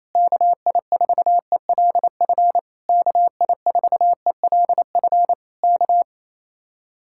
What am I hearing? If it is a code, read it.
Morse code, heard as a keyed tone, about 26 wpm, 700 Hz. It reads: KI4ELF KI4ELF K
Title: Speed-Racing Common QSO Phrases - 28wpm